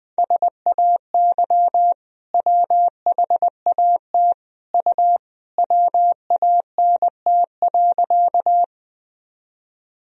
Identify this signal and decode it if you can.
Morse code, heard as a keyed tone, about 20 wpm, 700 Hz. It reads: SAY WHAT U WANT.
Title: Sentences from Top 200 Words and Abbreviations 20wpm